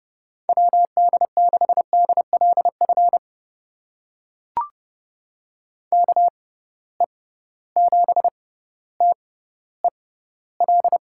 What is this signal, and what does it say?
Morse code, heard as a keyed tone, about 30 wpm, spaced as for 15 wpm, 700 Hz. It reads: WD6DLF KE7TEL
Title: USA Call Signs - 6x Character Spacing - 30wpm